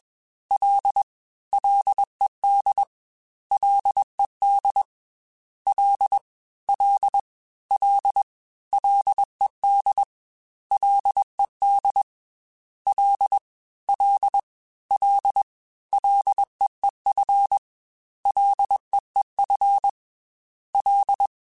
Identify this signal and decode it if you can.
Morse code, heard as a keyed tone, about 21 wpm, 800 Hz. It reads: L LED LED L L L LED LED L L L LEEF LEEF L
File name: F and L / F and L mp3 / F and L Prefixes 3x2 Twice @20